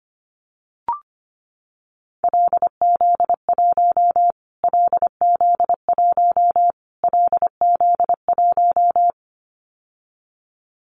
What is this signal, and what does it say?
Morse code, heard as a keyed tone, about 25 wpm, 700 Hz. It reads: LZ1 LZ1 LZ1
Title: Callsign Prefixes for Top 25 DXCC - Repeated 3x - 25wpm